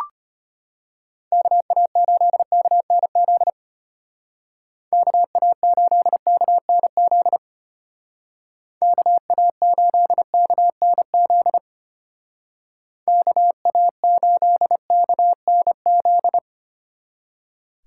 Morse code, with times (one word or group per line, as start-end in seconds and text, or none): 1.3-3.5 KA8KNZ
4.9-7.4 KA8KNZ
8.8-11.6 KA8KNZ
13.1-16.4 KA8KNZ